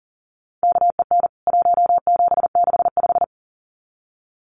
Morse code, heard as a keyed tone, about 40 wpm, 700 Hz. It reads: KEN 1765
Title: CWT Exchange 40wpm